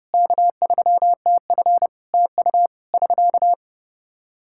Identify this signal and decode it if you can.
Morse code, heard as a keyed tone, about 30 wpm, 700 Hz. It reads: K3TF TU <SK>
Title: Practice QSO - Exchange 4 - Ending - 30wpm